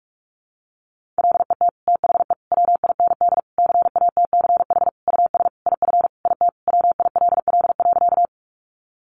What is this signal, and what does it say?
Morse code, heard as a keyed tone, about 44 wpm, 700 Hz. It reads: LET THE WIND CATCH US IF IT WILL.
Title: Sentences from Top 1500 Words 45wpm